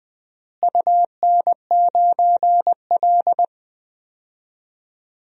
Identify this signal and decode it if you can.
Morse code, heard as a keyed tone, about 20 wpm, 700 Hz. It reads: UN9L
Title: DXCC-100 Call Signs - 4 Characters - Repeated 3x - 20wpm